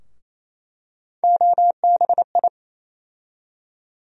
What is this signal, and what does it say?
Morse code, heard as a keyed tone, about 28 wpm, 700 Hz. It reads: OBI
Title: USA Call Sign Postfixes - 3 Letters 28wpm